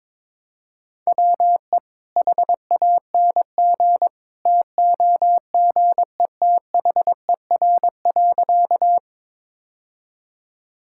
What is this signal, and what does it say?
Morse code, heard as a keyed tone, about 22 wpm, 700 Hz. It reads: WE HANG TOGETHER.
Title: Sentences from Top 1600 Words 22wpm